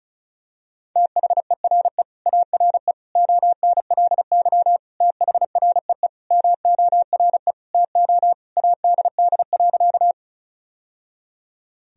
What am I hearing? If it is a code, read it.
Morse code, heard as a keyed tone, about 34 wpm, 700 Hz. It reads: THERE ARE ONLY THREE MORE TO ADD.
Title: Sentences from Top 1000 Words - Review Part 1 - 35wpm